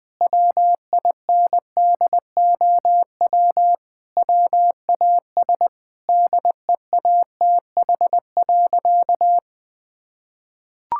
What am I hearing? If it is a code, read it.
Morse code, heard as a keyed tone, about 20 wpm, 700 Hz. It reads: WINDOW WAS DEATH.
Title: Sentences from Top 1600 Words 20wpm